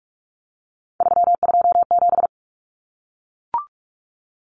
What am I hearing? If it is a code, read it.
Morse code, heard as a keyed tone, about 44 wpm, 700 Hz. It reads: 327
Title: Common Cut Numbers - 3-Digit 45wpm